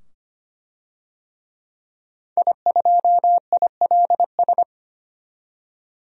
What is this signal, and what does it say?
Morse code, heard as a keyed tone, about 25 wpm, 700 Hz. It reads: I2ILS
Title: DXCC-100 Call Signs - 5 Characters - 25wpm